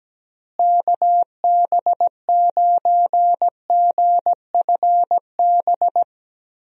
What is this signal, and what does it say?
Morse code, heard as a keyed tone, about 17 wpm, 700 Hz. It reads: KB9GFB
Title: USA Call Signs - 6 Characters - Repeated 2x - 17wpm